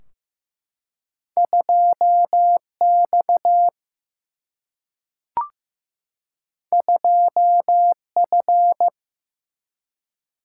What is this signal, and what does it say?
Morse code, heard as a keyed tone, about 15 wpm, 700 Hz. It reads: 2X 2F
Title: Single-Letter Single-Number 15wpm